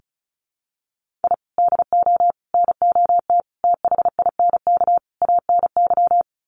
Morse code, heard as a keyed tone, about 34 wpm, 700 Hz. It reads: I DO NOT THINK ANY
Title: Sets of 5 Words Encore 35wpm